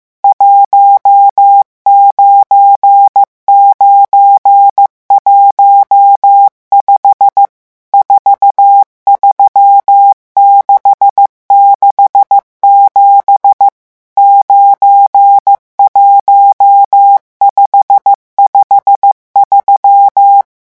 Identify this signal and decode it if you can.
Morse code, heard as a keyed tone, about 15 wpm, 800 Hz. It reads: 19915 43667 91553
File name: morse mp3 / revision11 chiffres vitesse 14